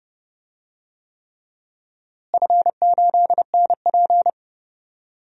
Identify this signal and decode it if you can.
Morse code, heard as a keyed tone, about 31 wpm, 700 Hz. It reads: F8NP
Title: DXCC-100 Call Signs - 4 Characters - Repeated 3x - 30wpm